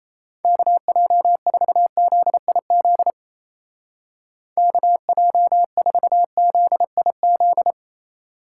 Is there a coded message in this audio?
Yes: Morse code from 0.4 to 7.7 s, heard as a keyed tone, about 36 wpm, 700 Hz. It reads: KJ4ZIZ KJ4ZIZ